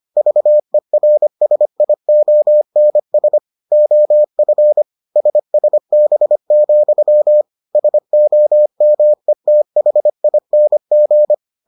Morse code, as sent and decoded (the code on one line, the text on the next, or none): ...- . .-. ... .. --- -. ... / --- ..-. / ... ... -... --..-- / ... --- -- . - .... .. -. --.
VERSIONS OF SSB, SOMETHING